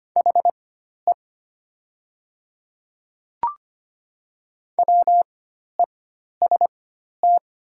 Morse code, heard as a keyed tone, about 25 wpm, spaced as for 12 wpm, 690 Hz. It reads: HE WEST
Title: 4x Character Spacing - Top 500 Words - 25wpm